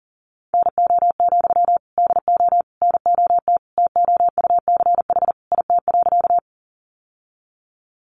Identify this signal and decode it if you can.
Morse code, heard as a keyed tone, about 40 wpm, 700 Hz. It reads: NO, DO NOT TOUCH IT.
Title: Sentences from Top 1200 Words 40wpm